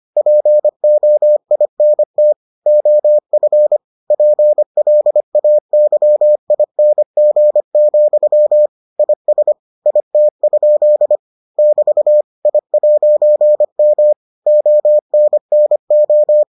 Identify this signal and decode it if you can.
Morse code, heard as a keyed tone, about 25 wpm, 600 Hz. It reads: POINT OF PLAYING, IS IT? <BT> I'M ONNO